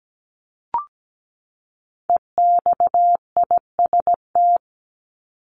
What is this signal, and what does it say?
Morse code, heard as a keyed tone, about 17 wpm, 700 Hz. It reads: EXIST
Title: ING Suffix - Top 100 Words with Base Words 17wpm